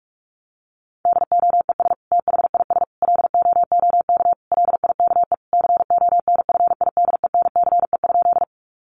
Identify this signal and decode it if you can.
Morse code, heard as a keyed tone, about 44 wpm, 700 Hz. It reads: DOES THIS LOOK LIKE CONFIDENCE?
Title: Sentences from Top 1500 Words 45wpm